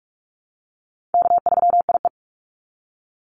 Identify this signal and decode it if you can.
Morse code, heard as a keyed tone, about 46 wpm, 700 Hz. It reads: K3IE
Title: Call Signs - 45wpm